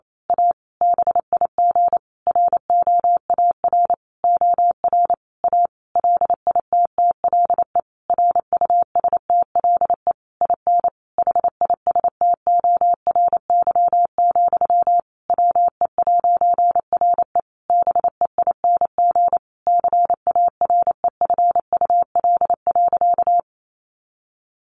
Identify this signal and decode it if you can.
Morse code, heard as a keyed tone, about 28 wpm, 700 Hz. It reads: A BIG ROAR OR A LITTLE RUSTLE IN HISTORY, WE'RE BEING CAREFUL.
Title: A Sound Of Thunder 28wpm